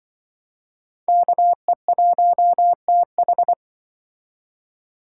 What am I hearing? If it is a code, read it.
Morse code, heard as a keyed tone, about 25 wpm, 700 Hz. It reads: KE1TH
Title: Speed Racing - USA Call Signs 20wpm